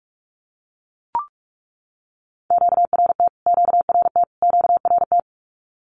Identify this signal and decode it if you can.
Morse code, heard as a keyed tone, about 44 wpm, 700 Hz. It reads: QRT QRT QRT
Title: Q-Codes Repeated 3x 45wpm